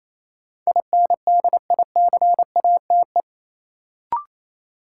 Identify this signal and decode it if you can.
Morse code, heard as a keyed tone, about 28 wpm, 700 Hz. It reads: INDICATE
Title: Sentences from Top 1300 Words - New Word Drill - 28wpm